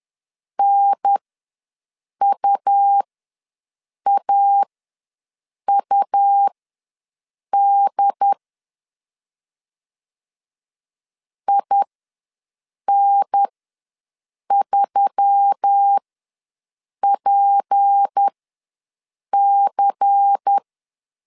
Morse code, heard as a keyed tone, about 10 wpm, spaced as for 6 wpm, 800 Hz. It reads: NUAUD IN3PC
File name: LEZIONI / 6 PKX0 / 6 P ws 10wpm 3751